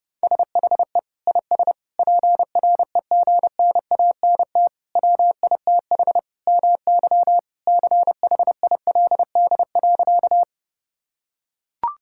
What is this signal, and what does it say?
Morse code, heard as a keyed tone, about 32 wpm, 700 Hz. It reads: SHE IS PREGNANT WITH MY CHILD.
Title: Sentences from Top 1000 Words - Review Part 6 - 30wpm